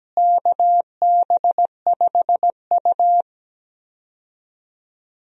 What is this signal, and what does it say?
Morse code, heard as a keyed tone, about 17 wpm, 700 Hz. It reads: KB5U
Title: USA Call Signs - 4 Characters - Repeated 3x - 17wpm